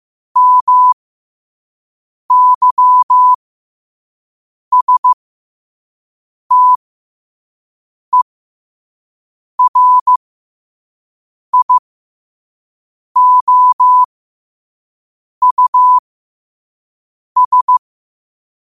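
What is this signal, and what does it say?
Morse code, heard as a keyed tone, about 15 wpm, spaced as for 5 wpm, 1000 Hz. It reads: MYSTERIOUS
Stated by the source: Morse code sent at 5 wpm for learning.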